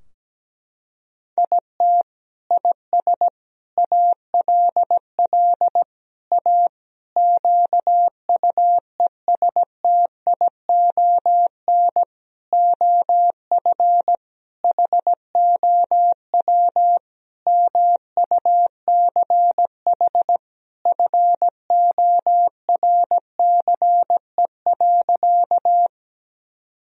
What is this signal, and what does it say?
Morse code, heard as a keyed tone, about 17 wpm, 700 Hz. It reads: IT IS ALL A QUESTION OF HOW MUCH FORCE.
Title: Sentences from Top 1000 Words - Review Part 3 - 17wpm